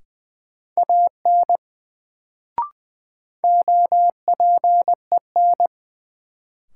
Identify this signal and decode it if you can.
Morse code, heard as a keyed tone, about 20 wpm, 700 Hz. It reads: AN OPEN
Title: Sentences from Top 300 Words - New Word Drill - 20wpm